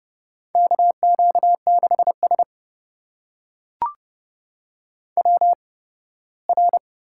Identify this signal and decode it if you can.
Morse code, heard as a keyed tone, about 30 wpm, spaced as for 18 wpm, 700 Hz. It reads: KQ6S WR
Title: USA Call Signs - 4 Characters - 8x Character Spacing - 30wpm